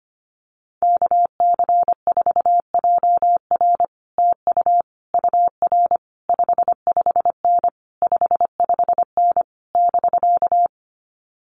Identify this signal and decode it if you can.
Morse code, heard as a keyed tone, about 25 wpm, 700 Hz. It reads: KC4JR TU UR 55N 55N <BK>
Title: Common QSO Phrases Encore 25wpm